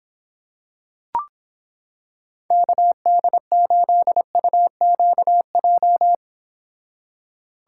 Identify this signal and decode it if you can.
Morse code, heard as a keyed tone, about 26 wpm, 700 Hz. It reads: KD8UQJ